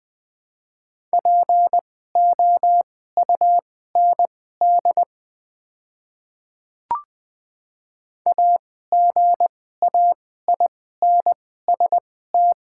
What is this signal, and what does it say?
Morse code, heard as a keyed tone, about 20 wpm, spaced as for 14 wpm, 700 Hz. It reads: POUND AGAINST
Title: 2x Character Spacing - Top 500 Words - 20wpm